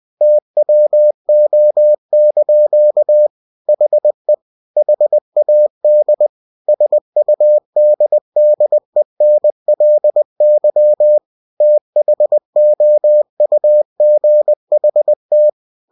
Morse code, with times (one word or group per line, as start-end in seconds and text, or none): 0.2-3.3 TWO)
3.7-4.3 HE
4.8-6.3 HAD
6.7-11.2 SUDDENLY
11.6-15.5 THOUGHT